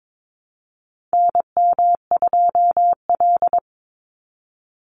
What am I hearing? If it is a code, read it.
Morse code, heard as a keyed tone, about 22 wpm, 700 Hz. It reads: NM2L
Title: Call Signs Repeated 3x - 22wpm